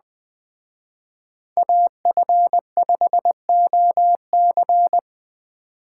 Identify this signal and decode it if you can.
Morse code, heard as a keyed tone, about 20 wpm, 700 Hz. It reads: AF5OC
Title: Speed Racing - USA 2x2 Call Signs 17wpm